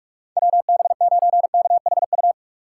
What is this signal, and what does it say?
Morse code, heard as a keyed tone, about 46 wpm, 700 Hz. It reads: WB9KHU
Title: USA Call Signs - 6 Characters - 6x Character Spacing - 45wpm